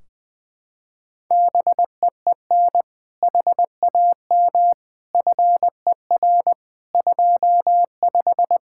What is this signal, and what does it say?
Morse code, heard as a keyed tone, about 20 wpm, 700 Hz. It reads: BEEN HAM FER 25
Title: K1USN SST QSO Practice 20wpm